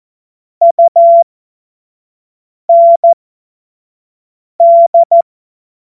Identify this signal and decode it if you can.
Morse code, heard as a keyed tone, about 14 wpm, spaced as for 5 wpm, 680 Hz. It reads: UND
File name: Witz-12-August-2025-25-F70